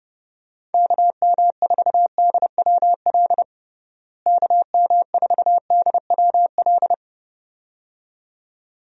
Morse code, heard as a keyed tone, about 32 wpm, 700 Hz. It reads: KM4DWL KM4DWL
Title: USA Call Signs - 6 Characters - Repeated 2x - 30wpm